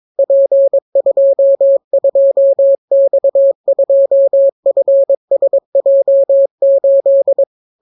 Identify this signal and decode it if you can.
Morse code, heard as a keyed tone, about 22 wpm, 550 Hz. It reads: P22X2FSJ8